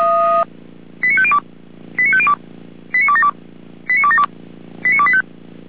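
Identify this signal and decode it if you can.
Five-tone selective call (ZVEI), heard as quick runs of five tones, tones of about 70 ms. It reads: 89502 89502 89272 89272 89285
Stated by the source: ZVEI five-tone selective call received on an SDR.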